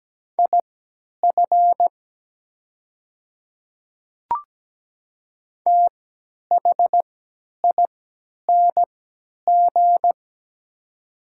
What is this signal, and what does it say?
Morse code, heard as a keyed tone, about 17 wpm, spaced as for 10 wpm, 700 Hz. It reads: IF THING
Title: 3x Character Spacing - Top 200 Words - 17wpm